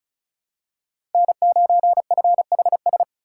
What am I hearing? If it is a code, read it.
Morse code, heard as a keyed tone, about 34 wpm, 700 Hz. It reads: N9FHS
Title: USA Call Signs - 2x Character Spacing - 35wpm